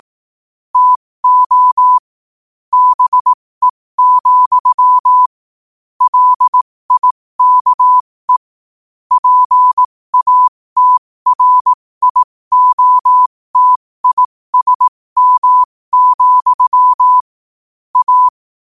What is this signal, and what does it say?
Morse code, heard as a keyed tone, about 18 wpm, spaced as for 15 wpm, 1000 Hz. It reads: TO BE, LIKE PATRIOTISM, A
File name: qotdfiles / 15wpm